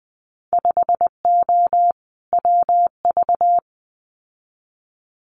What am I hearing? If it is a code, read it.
Morse code, heard as a keyed tone, about 20 wpm, 700 Hz. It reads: 5O WV
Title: Winter Field Day Exchanges Encore 20wpm